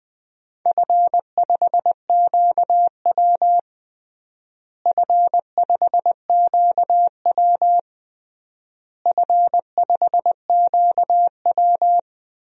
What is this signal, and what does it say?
Morse code, heard as a keyed tone, about 20 wpm, 700 Hz. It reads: F5QW F5QW F5QW
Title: DXCC-100 Call Signs - 4 Characters - Repeated 3x - 20wpm